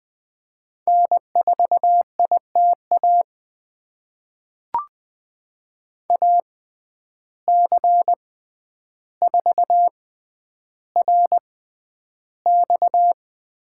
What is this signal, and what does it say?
Morse code, heard as a keyed tone, about 20 wpm, spaced as for 10 wpm, 700 Hz. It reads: N4ITA AC4RX
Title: USA Call Signs - 5 Characters - 6x Character Spacing - 20wpm